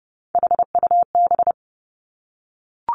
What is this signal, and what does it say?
Morse code, heard as a keyed tone, about 30 wpm, 700 Hz. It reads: HUB
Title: Three Challenging Letter-Number Combinations 30wpm